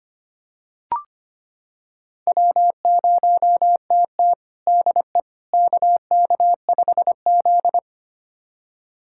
Morse code, heard as a keyed tone, about 25 wpm, 700 Hz. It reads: W0TT DE KK5Z
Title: USA Callsign DE Callsign - 4 Characters - 25wpm